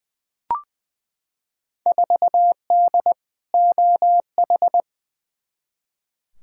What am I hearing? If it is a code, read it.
Morse code, heard as a keyed tone, about 20 wpm, 700 Hz. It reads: 4D OH
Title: Field Day Exchanges Encore - 20wpm